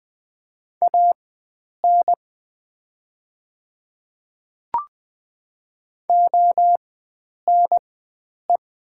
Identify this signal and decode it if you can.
Morse code, heard as a keyed tone, about 20 wpm, spaced as for 9 wpm, 700 Hz. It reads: AN ONE
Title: 4x Character Spacing - Top 100 Words - 20wpm